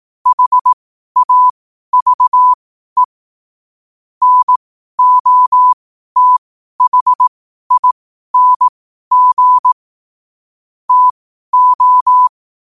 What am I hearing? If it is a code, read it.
Morse code, heard as a keyed tone, about 18 wpm, spaced as for 12 wpm, 1000 Hz. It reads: HAVE NOTHING TO
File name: qotdfiles / 12wpm